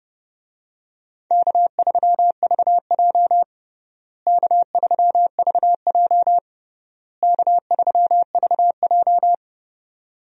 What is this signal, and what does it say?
Morse code, heard as a keyed tone, about 30 wpm, 700 Hz. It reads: K3VJ K3VJ K3VJ
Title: USA Call Signs - 4 Characters - Repeated 3x - 30wpm